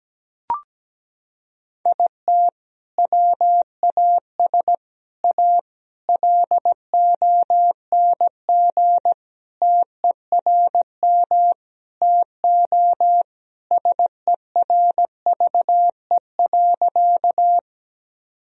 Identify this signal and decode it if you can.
Morse code, heard as a keyed tone, about 17 wpm, 700 Hz. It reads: IT WAS A LONG TERM TO SERVE.
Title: Sentences from Top 1000 Words - Review Part 4 - 17wpm